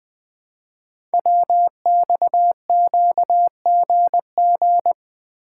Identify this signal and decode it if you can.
Morse code, heard as a keyed tone, about 20 wpm, 700 Hz. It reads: WXQGG
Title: Code Groups - 5 Letters - 20wpm